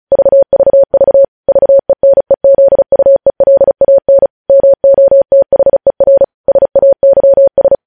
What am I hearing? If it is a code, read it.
Morse code, heard as a keyed tone, about 36 wpm, 550 Hz. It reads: VVV VENEZUELAN MOTHER SAYS
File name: podcast / News headlines-1x-35x35